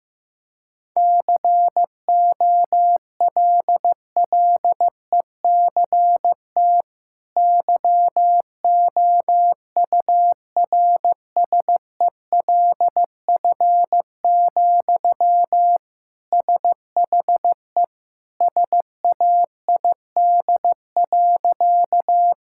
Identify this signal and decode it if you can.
Morse code, heard as a keyed tone, about 15 wpm, 700 Hz. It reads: COLLECT YOURSELF, SHE SAID.
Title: Sentences from Top 1600 Words 15wpm